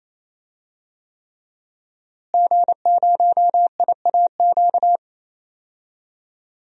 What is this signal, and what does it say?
Morse code, heard as a keyed tone, about 28 wpm, 700 Hz. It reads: G0IAQ
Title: DXCC-100 Call Signs - 5 Characters - Repeated 3x - 28wpm